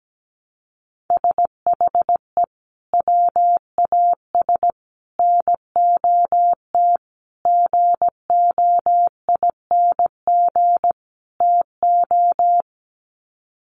Morse code, heard as a keyed tone, about 17 wpm, 700 Hz. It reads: SHE WAS NOT GOING TO
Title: Sets of 5 Words Encore2 17wpm